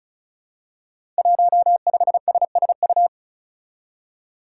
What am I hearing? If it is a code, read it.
Morse code, heard as a keyed tone, about 36 wpm, 700 Hz. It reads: 15SSU
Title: Five Challenging Letter-Number Combinations 35wpm